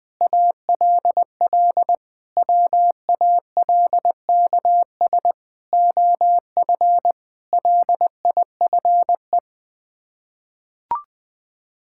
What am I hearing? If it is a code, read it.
Morse code, heard as a keyed tone, about 20 wpm, 700 Hz. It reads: ALL WALKS OF LIFE
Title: Sets of 6 Words - 20wpm